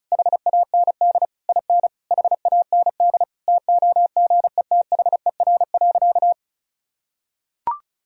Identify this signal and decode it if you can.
Morse code, heard as a keyed tone, about 36 wpm, 700 Hz. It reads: HAND IN HAND TOGETHER.
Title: Sentences from Top 1000 Words - Review Part 4 - 35wpm